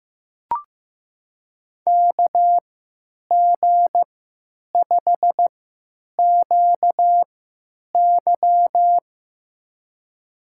Morse code, heard as a keyed tone, about 15 wpm, spaced as for 8 wpm, 700 Hz. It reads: KG5QY
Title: USA Call Signs - 5 Characters - 3x Character Spacing - 15wpm